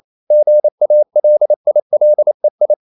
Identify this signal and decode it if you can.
Morse code, heard as a keyed tone, about 28 wpm, 600 Hz. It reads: GALILEI
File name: CW / Practice / EnglishQuotations0128